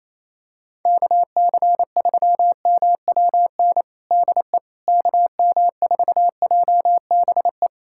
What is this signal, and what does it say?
Morse code, heard as a keyed tone, about 28 wpm, 700 Hz. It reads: KC3MWN DE KM4JBE
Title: USA Callsign DE Callsign - 6 Characters - 28wpm